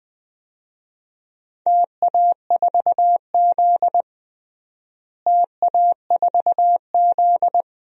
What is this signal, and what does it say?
Morse code, heard as a keyed tone, about 21 wpm, 700 Hz. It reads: TA4Z TA4Z
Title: DXCC-100 Call Signs - 4 Characters - Repeated 2x - 20wpm